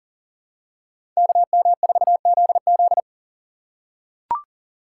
Morse code, heard as a keyed tone, about 40 wpm, 700 Hz. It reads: KM4ZZ
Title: K1USN SST QSO Practice - 40wpm